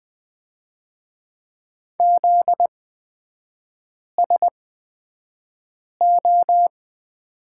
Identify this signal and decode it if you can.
Morse code, heard as a keyed tone, about 20 wpm, spaced as for 5 wpm, 700 Hz. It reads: ZSO
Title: 255 Lesson 41e - Period - Three Characters 20wpm